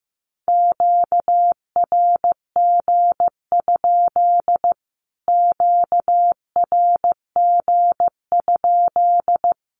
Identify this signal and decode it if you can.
Morse code, heard as a keyed tone, about 14 wpm, 700 Hz. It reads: QRG? QRG?
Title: Q-Codes Repeated 3x 15wpm